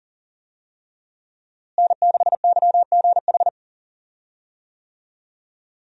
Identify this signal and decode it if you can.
Morse code, heard as a keyed tone, about 40 wpm, 700 Hz. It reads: N6YGH
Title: USA Call Signs - 5 Characters - Repeated 3x - 40wpm